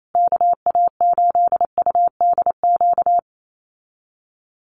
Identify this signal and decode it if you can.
Morse code, heard as a keyed tone, about 28 wpm, 700 Hz. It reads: KA8UDQ
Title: USA Call Signs Repeated 3x 28wpm